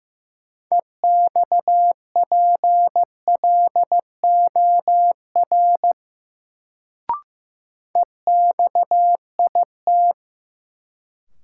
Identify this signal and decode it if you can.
Morse code, heard as a keyed tone, about 15 wpm, 700 Hz. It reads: EXPLOR EXIT
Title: Sentences from Top 1100 Words - New Word Drill Alt - 15wpm